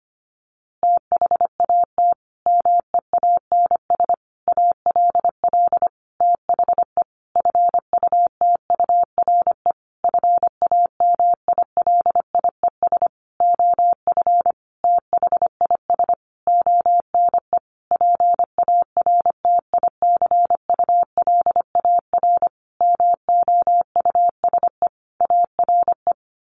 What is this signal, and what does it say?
Morse code, heard as a keyed tone, about 25 wpm, 700 Hz. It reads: THAT MEANS ALL THE FUTURE FAMILIES OF THIS ONE PARTICULAR MOUSE ARE
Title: A Sound Of Thunder 25wpm